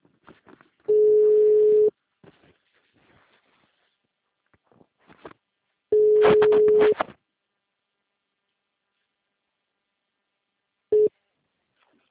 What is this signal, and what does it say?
Ringback tone (Europe, 425 Hz).